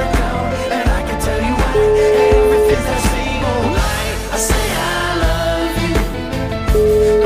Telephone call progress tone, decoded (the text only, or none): ringback tone